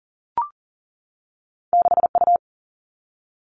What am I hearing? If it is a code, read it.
Morse code, heard as a keyed tone, about 39 wpm, 700 Hz. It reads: 6U
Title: All Cut Numbers - 2-Digit 40wpm